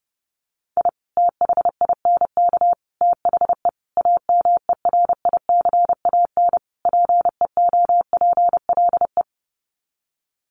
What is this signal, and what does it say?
Morse code, heard as a keyed tone, about 28 wpm, 700 Hz. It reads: I THINK THE AMERICAN PEOPLE
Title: Sets of 5 Words Encore 30wpm